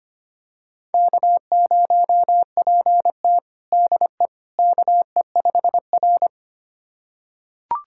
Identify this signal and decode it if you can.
Morse code, heard as a keyed tone, about 25 wpm, 700 Hz. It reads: K0PT DE KE5R
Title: USA Callsign DE Callsign - 4 Characters - 25wpm